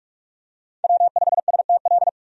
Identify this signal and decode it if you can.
Morse code, heard as a keyed tone, about 47 wpm, 700 Hz. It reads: W5STL
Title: USA Call Signs - 5 Characters - 45wpm